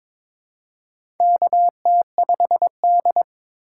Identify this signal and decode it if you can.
Morse code, heard as a keyed tone, about 21 wpm, 700 Hz. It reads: KT5D
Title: USA Call Signs - 4 Characters - 2x Character Spacing - 22wpm